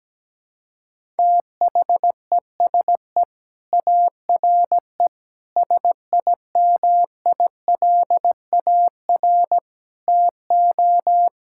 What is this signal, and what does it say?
Morse code, heard as a keyed tone, about 17 wpm, 700 Hz. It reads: THESE ARE SIMILAR TO
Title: Sentences from Top 1000 Words - Review Part 5 - 17wpm